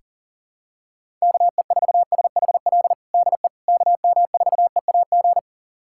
Morse code, heard as a keyed tone, about 40 wpm, 700 Hz. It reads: KE4SHL DE KM4EAG
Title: USA Callsign DE Callsign - 6 Characters - 40wpm